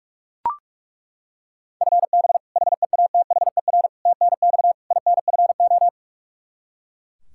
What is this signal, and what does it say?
Morse code, heard as a keyed tone, about 42 wpm, 700 Hz. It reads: FB HEATHER TNX INFO
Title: Practice QSO - Exchange 3 - More Info - 45wpm